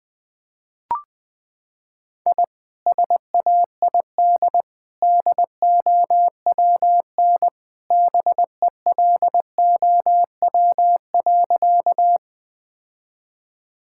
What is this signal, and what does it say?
Morse code, heard as a keyed tone, about 20 wpm, 700 Hz. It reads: I SAID DOWN BELOW.
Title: Sentences from Top 1000 Words - Review Part 5 - 20wpm